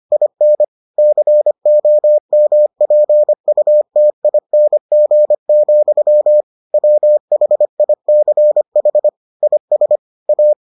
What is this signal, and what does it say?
Morse code, heard as a keyed tone, about 25 wpm, 600 Hz. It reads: IN COMPUTING, WHICH IS A